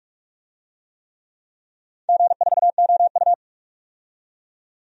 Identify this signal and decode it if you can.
Morse code, heard as a keyed tone, about 44 wpm, 700 Hz. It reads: G4OU
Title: DXCC-100 Call Signs - 4 Characters - 45wpm